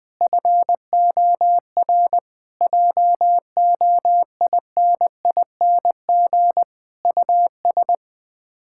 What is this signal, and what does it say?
Morse code, heard as a keyed tone, about 21 wpm, 700 Hz. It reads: FOR JOINING US
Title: Build A Phrase - 5-Word Phrases - 20wpm